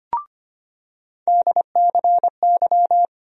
Morse code, heard as a keyed tone, about 25 wpm, 700 Hz. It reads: DCY
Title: USA Call Sign Postfixes - 3 Letters 25wpm